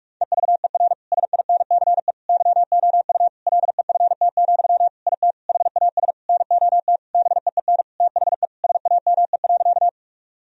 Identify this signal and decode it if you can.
Morse code, heard as a keyed tone, about 41 wpm, 700 Hz. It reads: EVER SINCE YOU LEFT, IT HAS NOT BEEN THE SAME.
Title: Sentences from Top 1000 Words - Review Part 1 - 45wpm